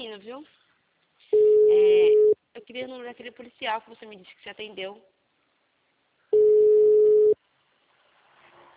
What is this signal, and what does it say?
Ringback tone (Europe, 425 Hz), with speech.